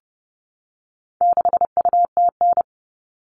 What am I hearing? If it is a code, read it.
Morse code, heard as a keyed tone, about 30 wpm, 700 Hz. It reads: 6UTN
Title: All Cut Numbers - 4-Digit 30wpm